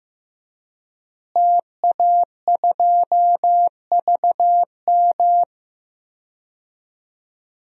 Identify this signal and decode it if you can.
Morse code, heard as a keyed tone, about 15 wpm, 700 Hz. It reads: TA2VM
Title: DXCC-100 Call Signs - 5 Characters - Repeated 3x - 15wpm